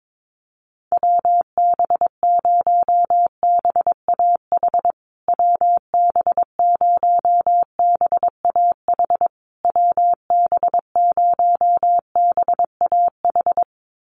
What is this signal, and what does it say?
Morse code, heard as a keyed tone, about 22 wpm, 700 Hz. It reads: WB0BAH WB0BAH WB0BAH
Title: USA Call Signs Repeated 3x 22wpm